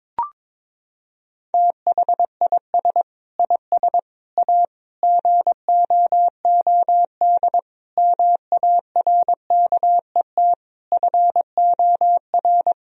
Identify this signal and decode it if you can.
Morse code, heard as a keyed tone, about 22 wpm, 700 Hz. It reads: THIS IS A GOOD MARKET FOR
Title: Sentences from Top 1000 Words - Review Part 3 - 22wpm